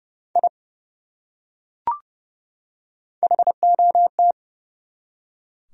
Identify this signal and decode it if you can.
Morse code, heard as a keyed tone, about 30 wpm, 700 Hz. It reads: I HOT
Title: Top 100 Words 30wpm